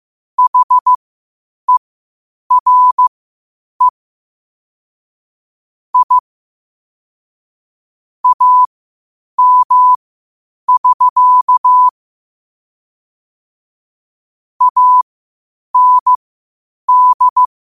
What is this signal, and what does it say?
Morse code, heard as a keyed tone, about 15 wpm, spaced as for 8 wpm, 1000 Hz. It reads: HERE I AM<SK> AND
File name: qotdfiles / 8wpm